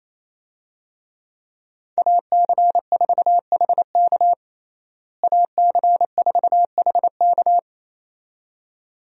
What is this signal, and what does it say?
Morse code, heard as a keyed tone, about 27 wpm, 700 Hz. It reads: AC4HK AC4HK
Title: USA Call Signs - 5 Characters - Repeated 2x - 28wpm